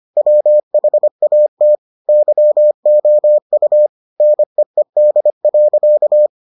Morse code, heard as a keyed tone, about 25 wpm, 600 Hz. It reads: WHAT YOU NEED.